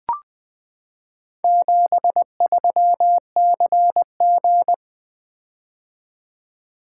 Morse code, heard as a keyed tone, about 21 wpm, 700 Hz. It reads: 73CG
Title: Four Letter-Number Combinations - 20wpm